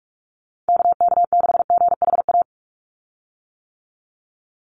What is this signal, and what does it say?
Morse code, heard as a keyed tone, about 44 wpm, 700 Hz. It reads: KK6GHA
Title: Call Signs - 45wpm